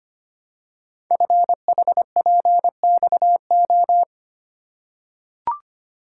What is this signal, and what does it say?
Morse code, heard as a keyed tone, about 25 wpm, 700 Hz. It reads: FHPXO
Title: Code Groups - 5 Letters - 25wpm